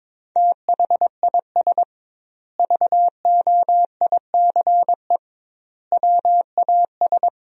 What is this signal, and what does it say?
Morse code, heard as a keyed tone, about 22 wpm, 700 Hz. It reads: THIS VOICE WAS
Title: 2x Word-Spacing - Sentences from Top 500 Words - 22wpm